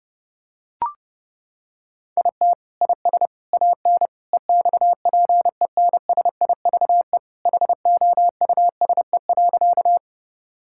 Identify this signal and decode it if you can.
Morse code, heard as a keyed tone, about 32 wpm, 700 Hz. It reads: IT IS AN EXPENSIVE HOUSE.
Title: Sentences from Top 1000 Words - Review Part 6 - 30wpm